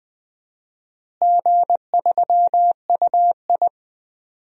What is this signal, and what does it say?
Morse code, heard as a keyed tone, about 20 wpm, 700 Hz. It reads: G3UI
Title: DXCC-100 Call Signs - 4 Characters - Repeated 2x - 20wpm